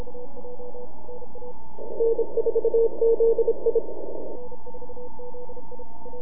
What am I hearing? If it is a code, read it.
Morse code, heard as a keyed tone, about 27 wpm, 460 Hz. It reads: N4ZI
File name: NAQPCW-8-2009 / W1UJ NAQP CW 09 1800Z